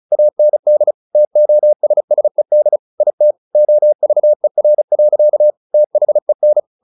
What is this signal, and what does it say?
Morse code, heard as a keyed tone, about 34 wpm, 600 Hz. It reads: AND TOSSED IT OVER. THEN
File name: the old man and the sea 2 35wpm